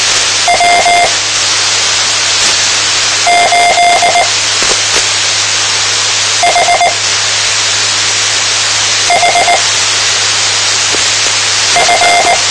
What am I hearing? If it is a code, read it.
Morse code, heard as a keyed tone, about 19 wpm, spaced as for 4 wpm, 700 Hz, with radio noise. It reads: W8HHF
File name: tmranets / Elmer 04 19 2015